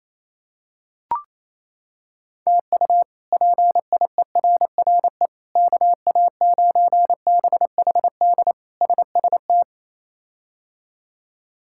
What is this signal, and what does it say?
Morse code, heard as a keyed tone, about 28 wpm, 700 Hz. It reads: TU PIERRE KA9BHD SST
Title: K1USN SST QSO Practice - 28wpm